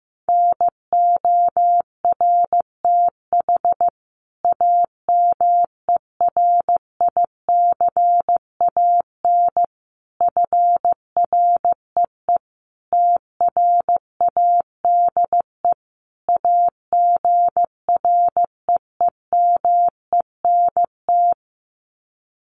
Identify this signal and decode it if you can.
Morse code, heard as a keyed tone, about 15 wpm, 700 Hz. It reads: NORTH AMERICAN FREE TRADE AGREEMENT
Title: Sets of 5 Words Encore 15wpm